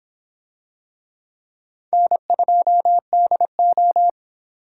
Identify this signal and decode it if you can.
Morse code, heard as a keyed tone, about 25 wpm, 700 Hz. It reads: N2DO